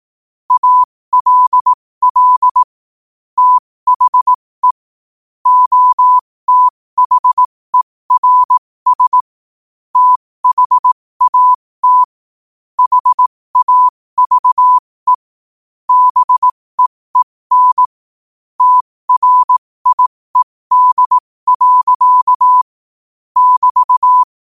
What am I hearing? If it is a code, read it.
Morse code, heard as a keyed tone, about 18 wpm, spaced as for 15 wpm, 1000 Hz. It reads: ALL THE OTHERS THAT HAVE BEEN TRIED. <BT>
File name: qotdfiles / 15wpm